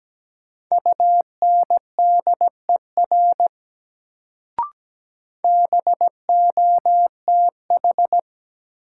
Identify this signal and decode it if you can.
Morse code, heard as a keyed tone, about 17 wpm, 700 Hz. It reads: UNDER BOTH
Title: Sentences from Top 300 Words - New Word Drill - 17wpm